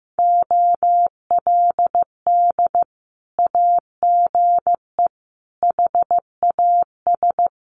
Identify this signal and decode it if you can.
Morse code, heard as a keyed tone, about 14 wpm, 700 Hz. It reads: OLD AGE HAS
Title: Sentences from Top 1100 Words 15wpm